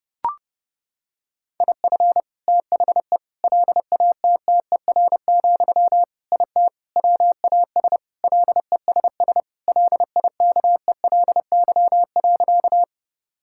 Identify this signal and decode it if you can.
Morse code, heard as a keyed tone, about 30 wpm, 700 Hz. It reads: IF THE LATTER, IT WAS LESS LIKELY.
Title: Sentences from Top 1600 Words 30wpm